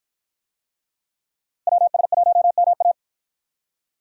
Morse code, heard as a keyed tone, about 53 wpm, 700 Hz. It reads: WS1MA